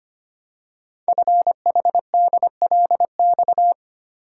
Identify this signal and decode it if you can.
Morse code, heard as a keyed tone, about 25 wpm, 700 Hz. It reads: FHDLX
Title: Code Groups - 5 Letter:Numbers - 25wpm